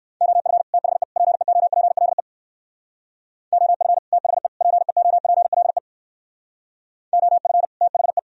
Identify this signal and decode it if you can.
Morse code, heard as a keyed tone, about 58 wpm, 700 Hz. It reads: OF THE PEOPLE OF THE PEOPLE OF THE